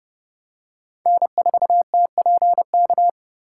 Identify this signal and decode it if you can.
Morse code, heard as a keyed tone, about 31 wpm, 700 Hz. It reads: N4TPK
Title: USA Call Signs - 4x Character Spacing - 30wpm